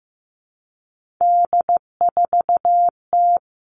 Morse code, heard as a keyed tone, about 15 wpm, 700 Hz. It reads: D4T
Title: All Cut Numbers - 3-Digit 15wpm